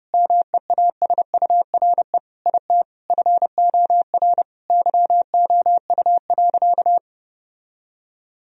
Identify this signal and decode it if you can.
Morse code, heard as a keyed tone, about 30 wpm, 700 Hz. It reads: MEASURE IT FOR YOU.
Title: Sentences from Top 1000 Words - Review Part 2 - 30wpm